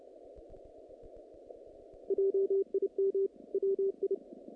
Morse code, heard as a keyed tone, about 30 wpm, 390 Hz. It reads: JIM WI
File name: contests / naqp-cw-k5tr-op-k5pi- -2100